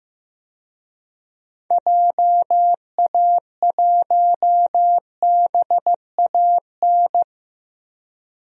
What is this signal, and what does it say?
Morse code, heard as a keyed tone, about 15 wpm, 700 Hz. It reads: JA1BAN
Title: DXCC-100 Call Signs - 6 Characters - Repeated 2x - 15wpm